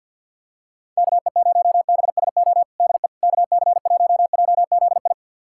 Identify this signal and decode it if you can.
Morse code, heard as a keyed tone, about 51 wpm, 700 Hz. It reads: KE0BSO DE KC1JZI
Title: USA Callsign DE Callsign - 6 Characters - 50wpm